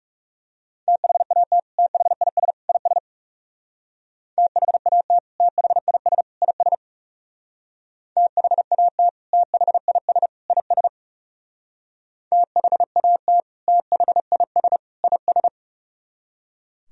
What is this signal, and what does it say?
Morse code, heard as a keyed tone, about 36 wpm, 700 Hz. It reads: THAT THIS IS THAT THIS IS THAT THIS IS THAT THIS IS
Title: Speed-Racing - Sets of 3 Words 30wpm